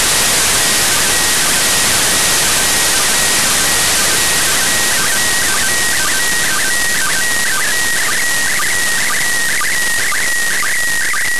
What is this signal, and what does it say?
SSTV (slow-scan television), PD 120.
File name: practicas-en-abierto / decodifica-imagenes-iss